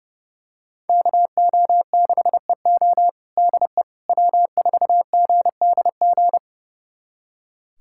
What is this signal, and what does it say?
Morse code, heard as a keyed tone, about 32 wpm, 700 Hz. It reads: KO6EO DE W4GDG
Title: USA Callsign DE Callsign - 5 Characters - 30wpm